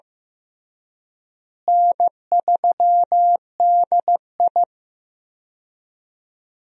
Morse code, heard as a keyed tone, about 15 wpm, 700 Hz. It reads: N3DI
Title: USA Call Signs - 4 Characters - Repeated 3x - 15wpm